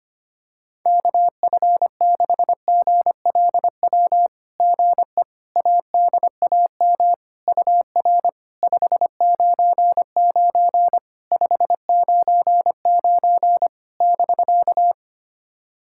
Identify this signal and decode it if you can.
Morse code, heard as a keyed tone, about 25 wpm, 690 Hz. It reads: KF6GLW GE ADAM UR 599 599 <BK>
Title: SOTA QSO Practice 25wpm